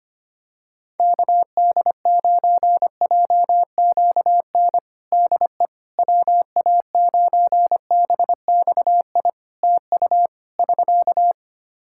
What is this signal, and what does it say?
Morse code, heard as a keyed tone, about 25 wpm, 700 Hz. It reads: KD9JQN DE WA9BXI TU <SK>
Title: Practice QSO - Exchange 4 - Ending - 25wpm